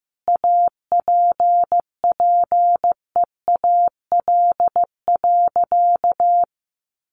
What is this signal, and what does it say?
Morse code, heard as a keyed tone, about 15 wpm, 700 Hz. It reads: APPEAL.
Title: Sentences from Top 1300 Words 15wpm